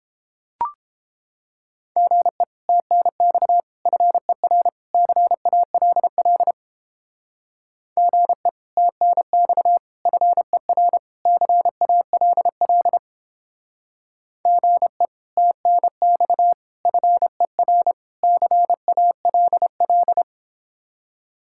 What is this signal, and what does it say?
Morse code, heard as a keyed tone, about 29 wpm, 700 Hz. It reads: GE TNX FER CALL GE TNX FER CALL GE TNX FER CALL